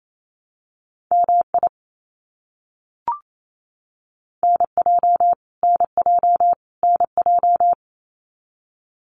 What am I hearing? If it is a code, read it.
Morse code, heard as a keyed tone, about 28 wpm, 700 Hz. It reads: MI NJ NJ NJ
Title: US States Repeated 3x 28wpm